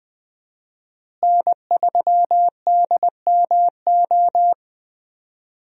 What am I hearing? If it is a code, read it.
Morse code, heard as a keyed tone, about 20 wpm, 700 Hz. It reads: N3DMO
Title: USA Call Signs - 5 Characters - Repeated 3x - 20wpm